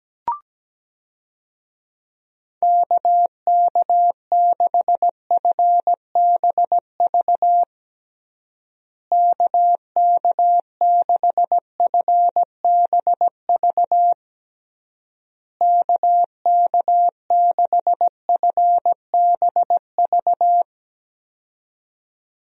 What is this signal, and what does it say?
Morse code, heard as a keyed tone, about 17 wpm, 700 Hz. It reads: KK6FBV KK6FBV KK6FBV
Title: USA Call Signs - 6 Characters - Repeated 3x - 17wpm